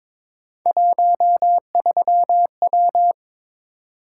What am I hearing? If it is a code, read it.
Morse code, heard as a keyed tone, about 22 wpm, 700 Hz. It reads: 13W
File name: practice / 22wpm / Three Letter-Number Combinations - 22wpm